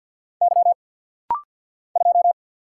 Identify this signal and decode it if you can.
Morse code, heard as a keyed tone, about 49 wpm, 700 Hz. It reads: Y 2
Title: Single Letter-Number Alt - Rapid-Fire 50wpm